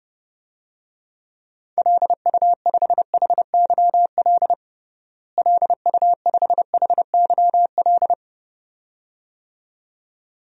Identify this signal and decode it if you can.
Morse code, heard as a keyed tone, about 30 wpm, 700 Hz. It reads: LU5HYL LU5HYL
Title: DXCC-100 Call Signs - 6 Characters - Repeated 2x - 30wpm